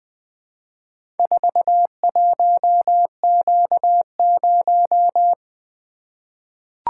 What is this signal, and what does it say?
Morse code, heard as a keyed tone, about 20 wpm, 700 Hz. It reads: 41Q0
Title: Four Letter-Number Combinations - 20wpm